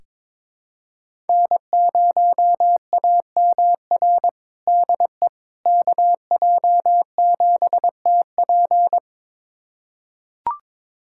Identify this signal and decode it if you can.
Morse code, heard as a keyed tone, about 22 wpm, 700 Hz. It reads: N0AMR DE KJ7TP
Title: USA Callsign DE Callsign - 5 Characters - 22wpm